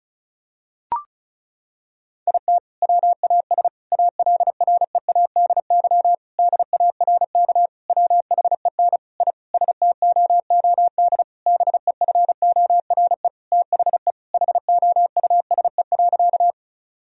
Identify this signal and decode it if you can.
Morse code, heard as a keyed tone, about 36 wpm, 700 Hz. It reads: IT WAS ALREADY DARK WHEN I STOOD BEFORE THE HOUSE.
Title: Sentences from Top 1000 Words - Review Part 2 - 35wpm